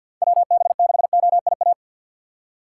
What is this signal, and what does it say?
Morse code, heard as a keyed tone, about 53 wpm, 700 Hz. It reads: WB6OIA
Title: USA Call Signs - 6 Characters - 8x Character Spacing - 50wpm